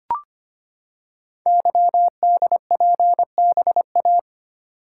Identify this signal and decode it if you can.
Morse code, heard as a keyed tone, about 25 wpm, 700 Hz. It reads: YDPBA
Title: Code Groups - 5 Letters - 25wpm